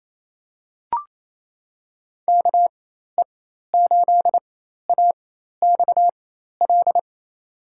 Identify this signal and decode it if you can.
Morse code, heard as a keyed tone, about 28 wpm, spaced as for 13 wpm, 700 Hz. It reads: KE8AXL
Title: USA Call Signs - 6 Characters - 4x Character Spacing - 28wpm